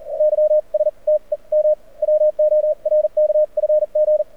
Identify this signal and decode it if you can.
Morse code, heard as a keyed tone, about 46 wpm, spaced as for 35 wpm, 590 Hz. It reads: YSTEM WORKFG
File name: archived / CRSnet